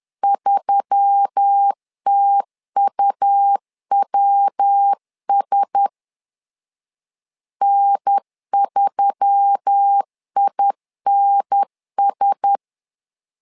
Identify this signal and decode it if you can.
Morse code, heard as a keyed tone, about 10 wpm, 800 Hz. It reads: 3TUWS N3INS
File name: LEZIONI / 4 Q1S3 10wpm 2427